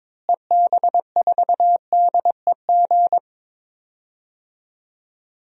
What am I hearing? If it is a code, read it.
Morse code, heard as a keyed tone, about 22 wpm, 700 Hz. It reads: EB4DEG
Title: DXCC-100 Call Signs - 6 Characters - Repeated 2x - 22wpm